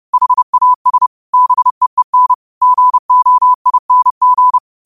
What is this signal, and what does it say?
Morse code, heard as a keyed tone, about 30 wpm, 1000 Hz. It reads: HAS BEEN GOING
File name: qotdfiles / 30wpm